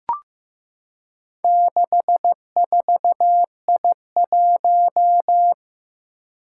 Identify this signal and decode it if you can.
Morse code, heard as a keyed tone, about 15 wpm, 700 Hz. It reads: 64I1
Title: Four Challenging Letter-Number Combinations 15wpm